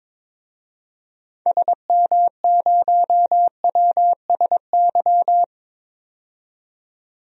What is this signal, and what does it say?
Morse code, heard as a keyed tone, about 22 wpm, 700 Hz. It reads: SM0WSY
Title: DXCC-100 Call Signs - 6 Characters - 22wpm